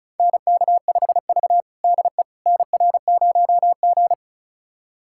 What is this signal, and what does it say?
Morse code, heard as a keyed tone, about 34 wpm, 700 Hz. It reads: NK5V DE NR0G
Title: USA Callsign DE Callsign - 4 Characters - 35wpm